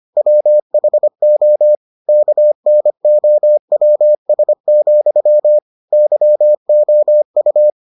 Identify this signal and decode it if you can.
Morse code, heard as a keyed tone, about 25 wpm, 600 Hz. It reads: WHO KNOWS, YOU